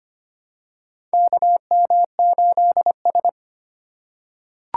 Morse code, heard as a keyed tone, about 25 wpm, 700 Hz. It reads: KM8S
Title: USA Call Signs - 4 Characters - 4x Character Spacing - 25wpm